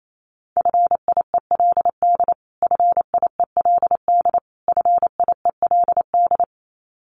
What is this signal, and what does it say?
Morse code, heard as a keyed tone, about 28 wpm, 700 Hz. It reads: FIELD FIELD FIELD
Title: Top 500 Words Repeated 28wpm